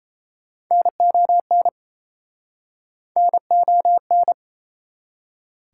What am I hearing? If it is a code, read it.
Morse code, heard as a keyed tone, about 38 wpm, 700 Hz. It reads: NON NON